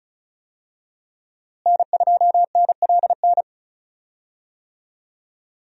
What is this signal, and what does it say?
Morse code, heard as a keyed tone, about 34 wpm, 700 Hz. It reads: N2NLN
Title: USA Call Signs - 5 Characters - Repeated 2x - 35wpm